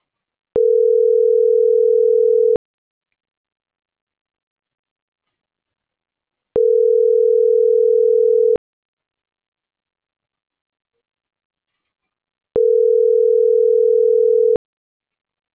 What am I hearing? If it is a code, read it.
Ringback tone (North America), 440 + 480 Hz.
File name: CENTRAL SAGOT / out miriam.mercado